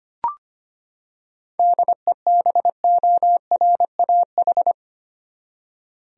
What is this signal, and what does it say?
Morse code, heard as a keyed tone, about 25 wpm, 700 Hz. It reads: DEBORAH
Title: USA Common Names - 25wpm